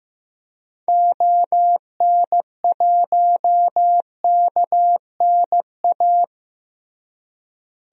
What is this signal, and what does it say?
Morse code, heard as a keyed tone, about 15 wpm, 700 Hz. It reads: ON1KNA
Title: DXCC-100 Call Signs - 6 Characters - Repeated 3x - 15wpm